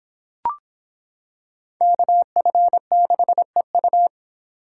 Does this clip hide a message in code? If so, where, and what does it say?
Yes: Morse code from 1.8 to 4.1 s, heard as a keyed tone, about 26 wpm, 700 Hz. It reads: KF6EU